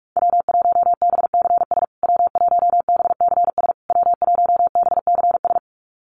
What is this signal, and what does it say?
Morse code, heard as a keyed tone, about 46 wpm, 690 Hz. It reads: W1BCS W1BCS W1BCS
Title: USA Call Signs Repeated 3x 45wpm